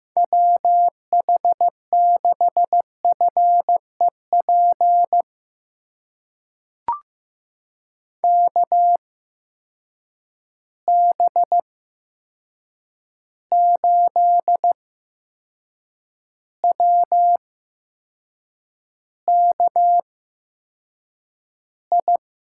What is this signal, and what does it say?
Morse code, heard as a keyed tone, about 15 wpm, spaced as for 6 wpm, 700 Hz. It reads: WH6FEP KB8WKI
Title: USA Call Signs - 6 Characters - 8x Character Spacing - 15wpm